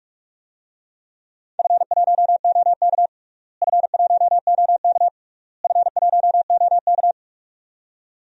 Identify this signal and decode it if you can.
Morse code, heard as a keyed tone, about 45 wpm, 700 Hz. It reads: F1OK F1OK F1OK
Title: DXCC-100 Call Signs - 4 Characters - Repeated 3x - 45wpm